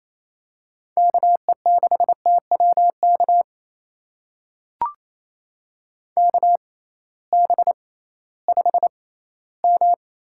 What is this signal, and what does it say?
Morse code, heard as a keyed tone, about 28 wpm, spaced as for 16 wpm, 700 Hz. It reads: KE6TWK KB5M
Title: USA Call Signs - 6 Characters - 6x Character Spacing - 28wpm